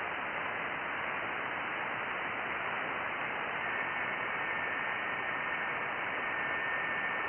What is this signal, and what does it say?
FT8.
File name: IC-7100